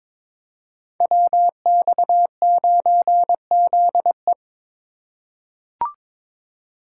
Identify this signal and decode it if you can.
Morse code, heard as a keyed tone, about 21 wpm, 700 Hz. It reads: WX9ZE
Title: Code Groups - 5 Letter:Numbers - 22wpm